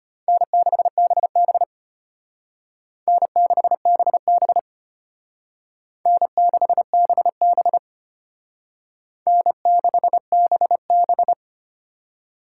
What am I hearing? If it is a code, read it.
Morse code, heard as a keyed tone, about 33 wpm, 700 Hz. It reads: N6BB N6BB N6BB N6BB